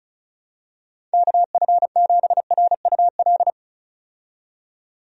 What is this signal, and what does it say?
Morse code, heard as a keyed tone, about 36 wpm, 700 Hz. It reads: KF7RUL
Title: USA Call Signs - 6 Characters - 35wpm